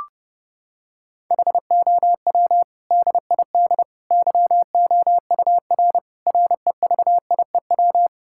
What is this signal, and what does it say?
Morse code, heard as a keyed tone, about 30 wpm, 700 Hz. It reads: HOW DID YOUR REVIEW
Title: Sentences from Top 1000 Words - Review Part 6 - 30wpm